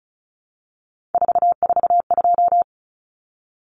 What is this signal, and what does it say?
Morse code, heard as a keyed tone, about 34 wpm, 700 Hz. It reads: 442
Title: Numbers - 3-Digit 35wpm